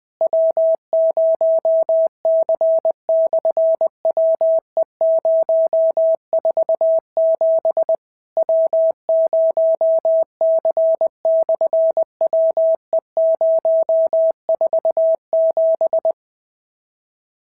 Morse code, heard as a keyed tone, about 20 wpm, 650 Hz. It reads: W0C/WE047 W0C/WE047
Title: SOTA QSO Practice 20wpm